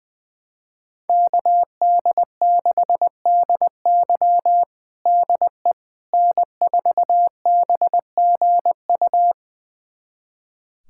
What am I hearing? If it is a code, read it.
Morse code, heard as a keyed tone, about 20 wpm, 700 Hz. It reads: KD6DY DE N4BGU
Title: USA Callsign DE Callsign - 5 Characters - 20wpm